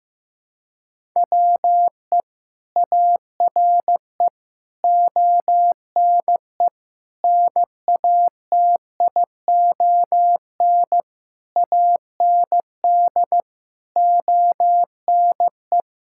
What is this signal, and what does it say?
Morse code, heard as a keyed tone, about 15 wpm, 700 Hz. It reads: WE ARE ONE NATION AND ONE
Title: Sentences from Top 1000 Words - Review Part 6 - 15wpm